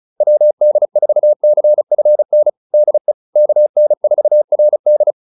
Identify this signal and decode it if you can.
Morse code, heard as a keyed tone, about 34 wpm, 600 Hz. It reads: WD4CFN DE KN4RD
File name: QSO207 35